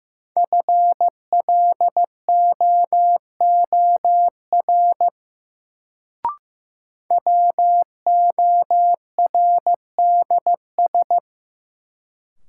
Morse code, heard as a keyed tone, about 15 wpm, 700 Hz. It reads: FLOOR WORDS
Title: Sets of 3 Words - New Word Drill - 15wpm